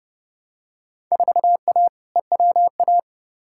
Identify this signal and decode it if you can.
Morse code, heard as a keyed tone, about 30 wpm, 700 Hz. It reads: 4A EWA
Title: Field Day Exchanges Encore - 30wpm